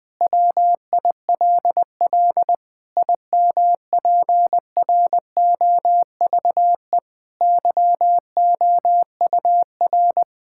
Morse code, heard as a keyed tone, about 21 wpm, 700 Hz. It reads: WILL IMPROVE YOUR
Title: Sentences from Top 1000 Words - Review Part 6 - 20wpm